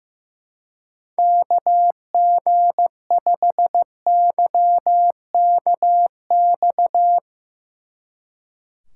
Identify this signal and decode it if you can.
Morse code, heard as a keyed tone, about 15 wpm, 700 Hz. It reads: KG5YKX
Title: Speed Racing - USA 2x3 Call Signs 15wpm